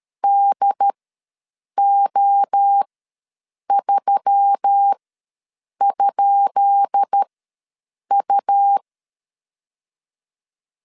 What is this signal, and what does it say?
Morse code, heard as a keyed tone, about 13 wpm, spaced as for 7 wpm, 800 Hz. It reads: DO3?U
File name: LEZIONI / 10 SIGNS / 10B INT ws 12wpm 3527